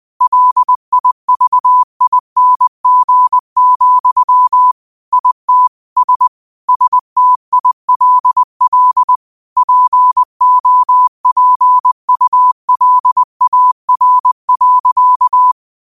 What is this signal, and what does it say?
Morse code, heard as a keyed tone, about 20 wpm, 1000 Hz. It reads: LIVING, IT S STILL POPULAR.
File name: 20wpm